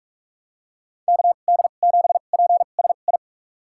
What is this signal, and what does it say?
Morse code, heard as a keyed tone, about 46 wpm, spaced as for 32 wpm, 700 Hz. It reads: KD7PSI
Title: USA Call Signs - 6 Characters - 2x Character Spacing - 45wpm